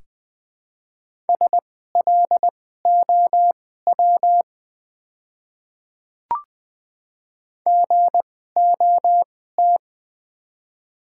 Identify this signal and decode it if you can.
Morse code, heard as a keyed tone, about 21 wpm, spaced as for 15 wpm, 700 Hz. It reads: SLOW GOT
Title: 2x Character Spacing - Top 500 Words - 20wpm